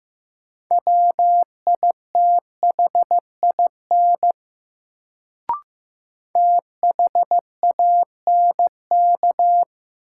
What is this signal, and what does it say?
Morse code, heard as a keyed tone, about 15 wpm, 700 Hz. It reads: WITHIN THANK
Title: Sets of 2 Words - New Word Drill - 15wpm